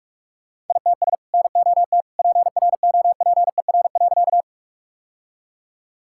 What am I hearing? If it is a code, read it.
Morse code, heard as a keyed tone, about 47 wpm, 700 Hz. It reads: ITS NOT PROPER.
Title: Sentences from Top 1000 Words - Review Part 5 - 45wpm